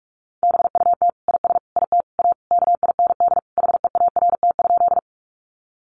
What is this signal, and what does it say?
Morse code, heard as a keyed tone, about 44 wpm, 700 Hz. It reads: BUT IS IT A KIND HEART?
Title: Sentences from Top 1200 Words 45wpm